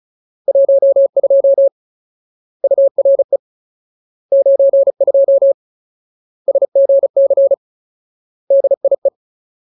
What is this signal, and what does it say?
Morse code, heard as a keyed tone, about 36 wpm, 550 Hz. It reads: 12 URE 92 SGC DIE
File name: Ultimate Word List / 35WPM Ultimate